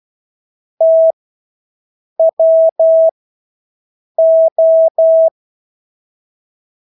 Morse code, heard as a keyed tone, about 12 wpm, spaced as for 6 wpm, 650 Hz. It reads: TWO
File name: Engelske ord 12-6 wpm